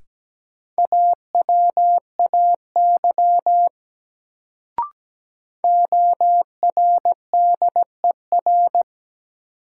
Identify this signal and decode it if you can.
Morse code, heard as a keyed tone, about 17 wpm, 700 Hz. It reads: AWAY ORDER
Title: Sets of 2 Words - New Word Drill - 17wpm